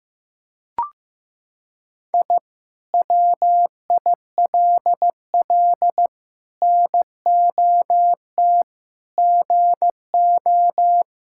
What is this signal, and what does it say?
Morse code, heard as a keyed tone, about 15 wpm, 700 Hz. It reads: I WILL NOT GO
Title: Sentences from Top 1000 Words - Review Part 6 - 15wpm